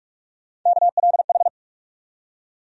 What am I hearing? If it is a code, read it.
Morse code, heard as a keyed tone, about 47 wpm, 700 Hz. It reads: KLH
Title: USA Call Sign Postfixes - 3 Letters 45wpm